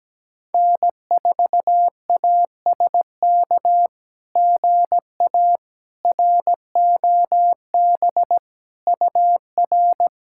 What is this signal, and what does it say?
Morse code, heard as a keyed tone, about 17 wpm, 700 Hz. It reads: N4ASK GA ROB UR
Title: SOTA QSO Practice 17wpm